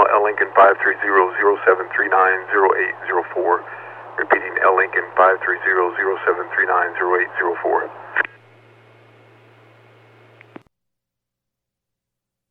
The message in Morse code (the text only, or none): none